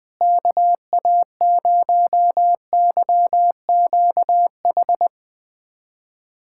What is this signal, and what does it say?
Morse code, heard as a keyed tone, about 20 wpm, 700 Hz. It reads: KA0YQH
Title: USA Callsign DE Callsign 20wpm